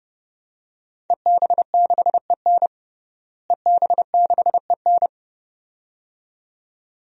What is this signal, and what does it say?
Morse code, heard as a keyed tone, about 30 wpm, 700 Hz. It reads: EB6EN EB6EN
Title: DXCC-100 Call Signs - 5 Characters - Repeated 2x - 30wpm